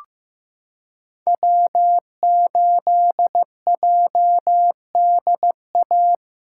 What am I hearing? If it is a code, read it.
Morse code, heard as a keyed tone, about 15 wpm, 700 Hz. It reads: W8JDA
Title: USA Callsign DE Callsign - 5 Characters - 15wpm